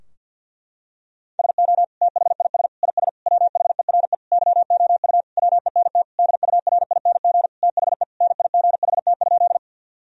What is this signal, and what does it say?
Morse code, heard as a keyed tone, about 50 wpm, 700 Hz. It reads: SO THIS IS WHERE YOU WENT DURING THE NIGHT?
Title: Sentences from Top 1000 Words - Review Part 2 - 50wpm